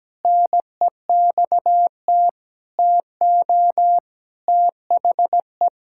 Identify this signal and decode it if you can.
Morse code, heard as a keyed tone, about 17 wpm, 700 Hz. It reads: NEXT TO THE
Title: Pangram Challenge v2 - 17wpm